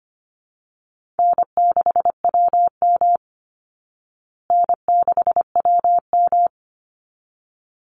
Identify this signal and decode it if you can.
Morse code, heard as a keyed tone, about 25 wpm, 700 Hz. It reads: N6WM N6WM
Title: Call Signs Repeated 3x - 25wpm